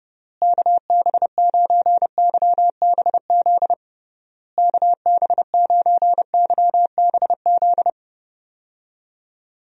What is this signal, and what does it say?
Morse code, heard as a keyed tone, about 32 wpm, 700 Hz. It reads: KB9YBZ KB9YBZ
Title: USA Call Signs - 6 Characters - Repeated 3x - 30wpm